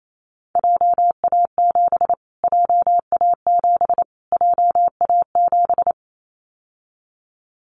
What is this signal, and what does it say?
Morse code, heard as a keyed tone, about 28 wpm, 700 Hz. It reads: JA7 JA7 JA7
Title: Callsign Prefixes for Top 25 DXCC - Repeated 3x - 28wpm